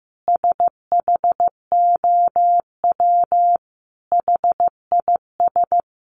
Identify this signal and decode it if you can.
Morse code, heard as a keyed tone, about 15 wpm, 700 Hz. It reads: SHOW HIS
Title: Sentences from Top 1200 Words 15wpm